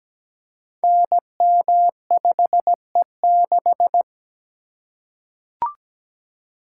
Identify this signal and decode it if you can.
Morse code, heard as a keyed tone, about 17 wpm, 700 Hz. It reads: NM5E6
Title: Code Groups - 5 Letter:Numbers - 17wpm